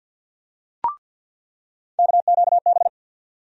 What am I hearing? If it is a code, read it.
Morse code, heard as a keyed tone, about 49 wpm, 700 Hz. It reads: KQB
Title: USA Call Sign Postfixes - 3 Letters 50wpm